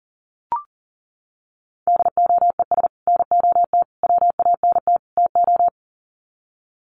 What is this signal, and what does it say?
Morse code, heard as a keyed tone, about 40 wpm, 700 Hz. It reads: DOES NOT WANT TO
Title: Sets of 4 Words 40wpm